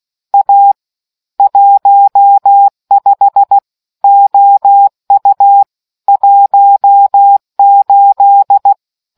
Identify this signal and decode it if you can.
Morse code, heard as a keyed tone, about 16 wpm, 800 Hz. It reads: A 15 OU 18